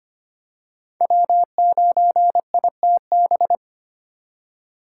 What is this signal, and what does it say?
Morse code, heard as a keyed tone, about 25 wpm, 700 Hz. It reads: W9ITB
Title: USA Call Signs - 5 Characters - 3x Character Spacing - 25wpm